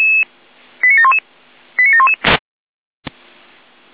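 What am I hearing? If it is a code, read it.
Five-tone selective call (ZVEI), heard as quick runs of five tones, tones of about 70 ms. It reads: 89711 89711
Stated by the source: ZVEI five-tone selective call received on an SDR, channel 455.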